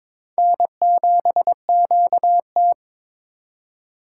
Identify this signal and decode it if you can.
Morse code, heard as a keyed tone, about 22 wpm, 700 Hz. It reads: N7QT
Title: K1USN SST QSO Practice - 22wpm